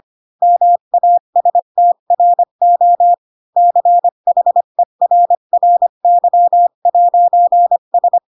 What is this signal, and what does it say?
Morse code, heard as a keyed tone, about 25 wpm, 700 Hz. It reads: MASTRO CHERRY'S
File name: Pinocchio / Pinocchio ch1 20 25 ik1hge